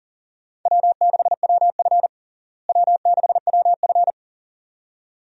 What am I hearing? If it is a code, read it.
Morse code, heard as a keyed tone, about 40 wpm, 700 Hz. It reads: W6WF W6WF
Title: USA Call Signs - 4 Characters - Repeated 2x - 40wpm